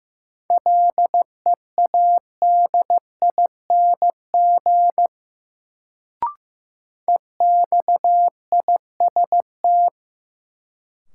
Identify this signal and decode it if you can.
Morse code, heard as a keyed tone, about 15 wpm, 700 Hz. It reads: LEADING EXIST
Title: Sentences from Top 1100 Words - New Word Drill Alt - 15wpm